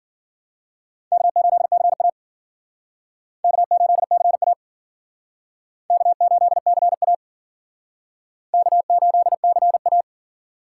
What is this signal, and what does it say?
Morse code, heard as a keyed tone, about 50 wpm, 700 Hz. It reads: K8CA K8CA K8CA K8CA
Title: Speed Racing - USA 1x2 Call Signs 40wpm